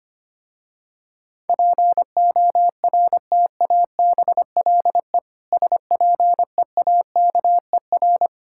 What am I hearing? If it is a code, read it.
Morse code, heard as a keyed tone, about 25 wpm, 700 Hz. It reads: PORTABLE SPEAKER
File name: practice / 25wpm / Difficult Words with Hints - 25wpm